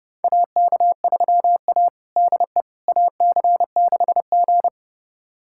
Morse code, heard as a keyed tone, about 32 wpm, 700 Hz. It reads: AK3A DE AC6G
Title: USA Callsign DE Callsign - 4 Characters - 30wpm